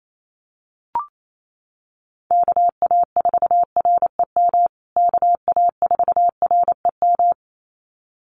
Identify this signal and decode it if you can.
Morse code, heard as a keyed tone, about 28 wpm, 700 Hz. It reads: KA4REM KA4REM
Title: USA Call Signs Repeated 2x 28wpm